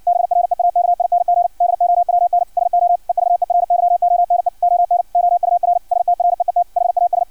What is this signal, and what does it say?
Morse code, heard as a keyed tone, about 56 wpm, 700 Hz. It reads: BREAKING DOWN SO EVERYONE ON OUR STREET HAS
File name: archived / CRSnet